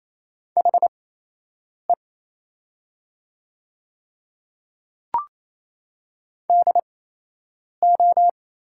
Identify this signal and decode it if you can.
Morse code, heard as a keyed tone, about 28 wpm, spaced as for 8 wpm, 700 Hz. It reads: HE DO
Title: 8x Character Spacing - Top 100 Words - 28wpm